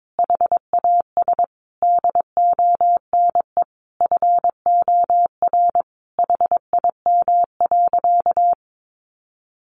Morse code, heard as a keyed tone, about 22 wpm, 700 Hz. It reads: HAS DONE FOR HIM.
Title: Sentences from Top 1400 Words 22wpm